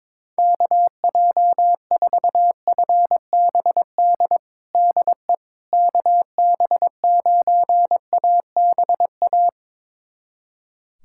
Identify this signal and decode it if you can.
Morse code, heard as a keyed tone, about 22 wpm, 700 Hz. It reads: KJ4FBD DE KB9ABA
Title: USA Callsign DE Callsign - 6 Characters - 22wpm